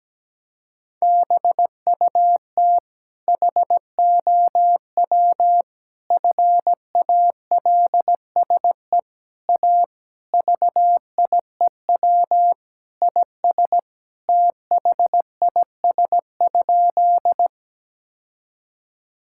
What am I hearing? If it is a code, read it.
Morse code, heard as a keyed tone, about 17 wpm, 700 Hz. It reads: BUT HOW FALSE A VIEW IS THIS?
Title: Sentences from Top 1600 Words 17wpm